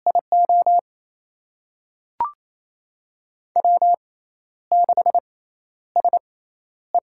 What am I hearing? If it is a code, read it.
Morse code, heard as a keyed tone, about 28 wpm, spaced as for 12 wpm, 700 Hz. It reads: IO W6SE
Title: USA Call Signs - 4 Characters - 6x Character Spacing - 28wpm